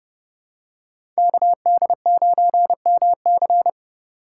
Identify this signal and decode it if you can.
Morse code, heard as a keyed tone, about 30 wpm, 700 Hz. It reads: KD9MC
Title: USA Call Signs - 5 Characters - 3x Character Spacing - 30wpm